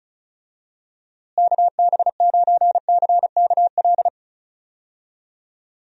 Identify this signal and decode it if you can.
Morse code, heard as a keyed tone, about 36 wpm, 700 Hz. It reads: KB9CKL
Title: USA Call Signs - 6 Characters - Repeated 2x - 35wpm